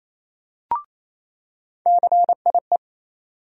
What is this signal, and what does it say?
Morse code, heard as a keyed tone, about 28 wpm, 700 Hz. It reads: CIE
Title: USA Call Sign Postfixes - 3 Letters 28wpm